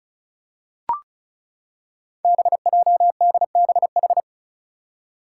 Morse code, heard as a keyed tone, about 36 wpm, 700 Hz. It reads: BJDBH
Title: Five Challenging Letter-Number Combinations 35wpm